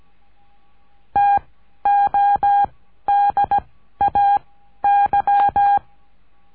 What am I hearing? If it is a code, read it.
Morse code, heard as a keyed tone, about 17 wpm, 810 Hz. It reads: T O D A Y
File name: cursos / CURSO DE TELEGRAFIA / Telegrafia - Trilha 14